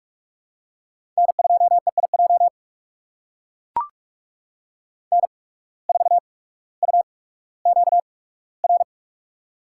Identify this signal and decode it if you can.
Morse code, heard as a keyed tone, about 43 wpm, spaced as for 19 wpm, 700 Hz. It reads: N2EIJ N4UQR
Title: USA Call Signs - 5 Characters - 8x Character Spacing - 45wpm